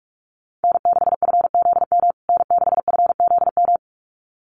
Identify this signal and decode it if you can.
Morse code, heard as a keyed tone, about 44 wpm, 700 Hz. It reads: N6FZM N6FZM
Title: USA Call Signs Repeated 2x 45wpm